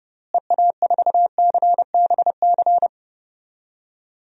Morse code, heard as a keyed tone, about 32 wpm, 700 Hz. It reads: EA4CBC
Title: DXCC-100 Call Signs - 6 Characters - Repeated 2x - 30wpm